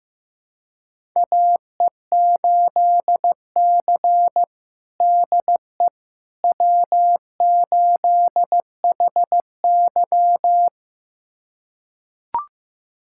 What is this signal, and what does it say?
Morse code, heard as a keyed tone, about 15 wpm, 700 Hz. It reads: AE8C DE W8HY
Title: USA Callsign DE Callsign - 4 Characters - 15wpm